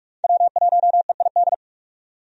Morse code, heard as a keyed tone, about 46 wpm, 700 Hz. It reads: W1EID
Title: USA Call Signs - 5 Characters - 45wpm